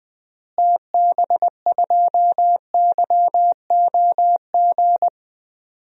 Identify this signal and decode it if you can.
Morse code, heard as a keyed tone, about 20 wpm, 700 Hz. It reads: TB2YOG
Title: DXCC-100 Call Signs - 6 Characters - Repeated 2x - 20wpm